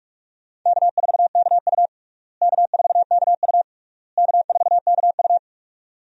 Morse code, heard as a keyed tone, about 46 wpm, 700 Hz. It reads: K4KU K4KU K4KU
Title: USA Call Signs - 4 Characters - Repeated 3x - 45wpm